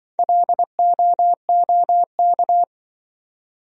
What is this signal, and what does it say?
Morse code, heard as a keyed tone, about 25 wpm, 700 Hz. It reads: LOOK
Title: Speed-Racing Top 100 Words 20wpm